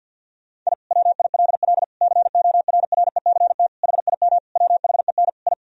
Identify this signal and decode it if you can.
Morse code, heard as a keyed tone, about 50 wpm, 700 Hz. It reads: I WILL CORRECT HIM WHEN I
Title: Sentences from Top 1000 Words - Review Part 2 - 50wpm